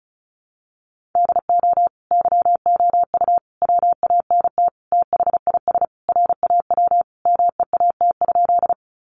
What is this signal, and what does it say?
Morse code, heard as a keyed tone, about 34 wpm, 700 Hz. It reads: DO YOU WANT THIS RAW MEAT?
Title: Sentences from Top 1500 Words 35wpm